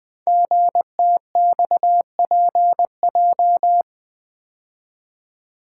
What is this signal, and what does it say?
Morse code, heard as a keyed tone, about 20 wpm, 700 Hz. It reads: GTXPJ
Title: Code Groups - 5 Letters - 20wpm